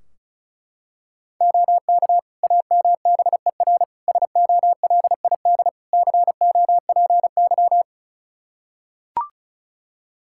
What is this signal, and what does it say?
Morse code, heard as a keyed tone, about 36 wpm, 700 Hz. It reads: OK AMBER SOLID COPY
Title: Practice QSO - Exchange 3 - More Info - 35wpm